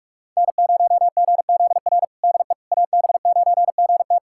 Morse code, heard as a keyed tone, about 47 wpm, 700 Hz. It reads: N0GZR DE AB9GT
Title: USA Callsign DE Callsign - 5 Characters - 45wpm